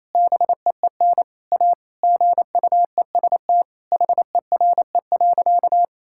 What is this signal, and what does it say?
Morse code, heard as a keyed tone, about 27 wpm, 700 Hz. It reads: BEEN A GUEST HERE.
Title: Sentences from Top 1600 Words 28wpm